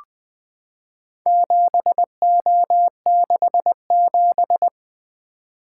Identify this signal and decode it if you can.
Morse code, heard as a keyed tone, about 20 wpm, 700 Hz. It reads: 7O67
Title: Four Letter-Number Combinations - 20wpm